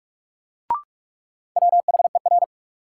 Morse code, heard as a keyed tone, about 47 wpm, 700 Hz. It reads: WHER
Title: Sentences from Top 1500 Words - New Word Drill - 45wpm